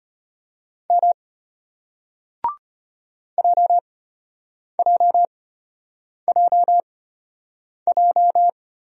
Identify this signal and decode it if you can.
Morse code, heard as a keyed tone, about 34 wpm, spaced as for 8 wpm, 700 Hz. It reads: M JJJJ